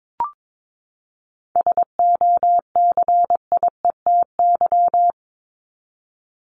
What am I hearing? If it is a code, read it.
Morse code, heard as a keyed tone, about 22 wpm, 700 Hz. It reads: SOCIETY
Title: Top 501-750 Words 22wpm